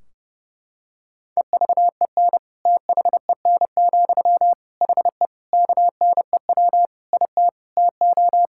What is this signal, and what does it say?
Morse code, heard as a keyed tone, about 30 wpm, 700 Hz. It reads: EVEN THEN, HE KNEW IT TO
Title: Sentences from Top 1600 Words 30wpm